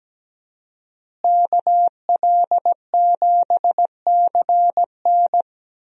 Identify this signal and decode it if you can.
Morse code, heard as a keyed tone, about 17 wpm, 700 Hz. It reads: KL7CN
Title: USA Call Signs - 5 Characters - Repeated 2x - 17wpm